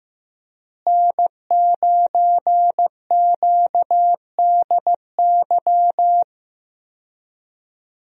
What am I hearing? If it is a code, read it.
Morse code, heard as a keyed tone, about 15 wpm, 700 Hz. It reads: N9QDY
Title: USA Call Signs - 5 Characters - Repeated 3x - 15wpm